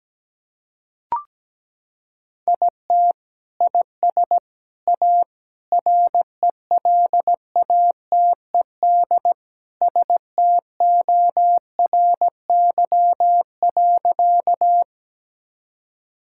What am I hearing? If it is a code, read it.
Morse code, heard as a keyed tone, about 17 wpm, 700 Hz. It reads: IT IS A RELATED STORY.
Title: Sentences from Top 1000 Words - Review Part 5 - 17wpm